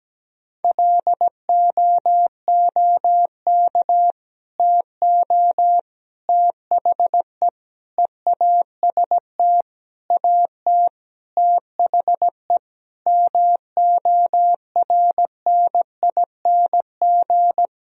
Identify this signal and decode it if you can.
Morse code, heard as a keyed tone, about 17 wpm, 700 Hz. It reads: LOOK TO THE EAST AT THE MORNING
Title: Sentences from Top 1000 Words - Review Part 3 - 17wpm